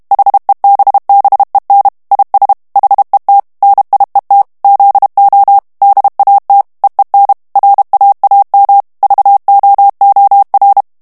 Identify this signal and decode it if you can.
Morse code, heard as a keyed tone, about 32 wpm, 800 Hz. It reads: HEBBEN IS HET NIET ZO DAT EEN RAAM VOOR
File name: HOLLANDAIS texte clair 1 vitesse 30 mots minute